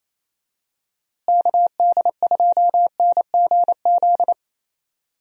Morse code, heard as a keyed tone, about 28 wpm, 700 Hz. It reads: KD2NGZ
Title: USA Call Signs - 6 Characters - Repeated 2x - 28wpm